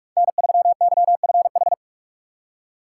Morse code, heard as a keyed tone, about 44 wpm, 700 Hz. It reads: N3YFH
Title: USA Call Signs - 5 Characters - 3x Character Spacing - 45wpm